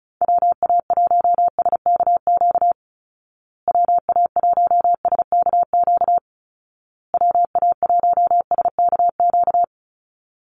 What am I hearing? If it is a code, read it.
Morse code, heard as a keyed tone, about 34 wpm, 700 Hz. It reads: WA1SKQ WA1SKQ WA1SKQ
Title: Call Signs Repeated 2x - 35wpm